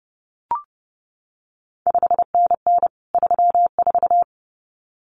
Morse code, heard as a keyed tone, about 30 wpm, 700 Hz. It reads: 5NN 34
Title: CQ WW DX Contest Exchange Encore - 30wpm